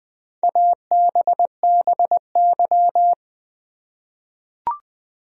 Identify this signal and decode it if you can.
Morse code, heard as a keyed tone, about 20 wpm, 700 Hz. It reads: ABBY
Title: USA Common Names - Max 5 Letters - 20wpm